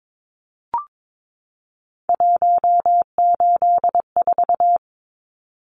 Morse code, heard as a keyed tone, about 22 wpm, 700 Hz. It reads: 184
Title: Numbers - 3-Digit 22wpm